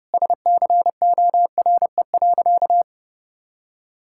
Morse code, heard as a keyed tone, about 30 wpm, 700 Hz. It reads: SCORE.
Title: Sentences from Top 1600 Words 30wpm